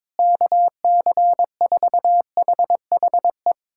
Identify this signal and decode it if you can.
Morse code, heard as a keyed tone, about 22 wpm, 700 Hz. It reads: KC4HHE
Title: USA Callsign DE Callsign 22wpm